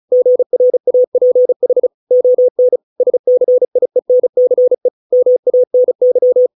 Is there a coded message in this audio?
Yes: Morse code from 0.1 to 6.5 s, heard as a keyed tone, about 35 wpm, 500 Hz. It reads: GRAPH ON SCIENCE MANY